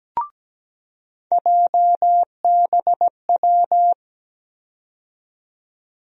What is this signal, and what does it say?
Morse code, heard as a keyed tone, about 17 wpm, 700 Hz. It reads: JBW
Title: USA Call Sign Postfixes - 3 Letters 17wpm